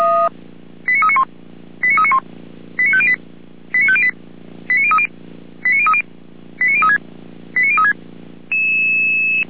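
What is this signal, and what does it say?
Five-tone selective call (ZVEI), heard as quick runs of five tones, tones of about 70 ms. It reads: 89391 89391 89508 89508 89030 89030 89036 89036
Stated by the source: Five-tone ZVEI selective call 89125, received by SDR.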